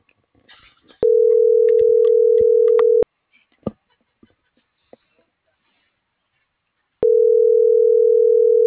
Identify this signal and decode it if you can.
Ringback tone (North America), 440 + 480 Hz.